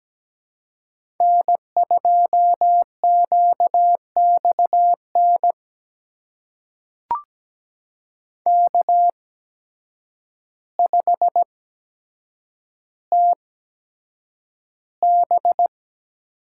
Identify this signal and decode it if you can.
Morse code, heard as a keyed tone, about 17 wpm, spaced as for 8 wpm, 700 Hz. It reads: N2QXN K5TB
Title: USA Call Signs - 5 Characters - 8x Character Spacing - 17wpm